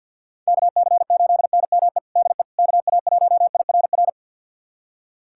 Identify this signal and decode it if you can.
Morse code, heard as a keyed tone, about 48 wpm, 700 Hz. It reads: KC8NME DE KA1IRR
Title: USA Callsign DE Callsign - 6 Characters - 50wpm